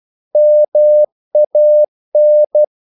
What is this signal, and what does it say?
Morse code, heard as a keyed tone, about 12 wpm, 600 Hz. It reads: MAN
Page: English quotations in CW (telegraphy - morse)